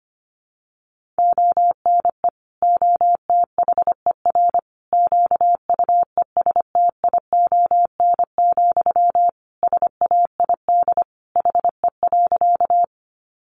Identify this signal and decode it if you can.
Morse code, heard as a keyed tone, about 25 wpm, 700 Hz. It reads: ONE OTHER QUESTION, SAID HE.
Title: Sentences from Top 400 Words 25wpm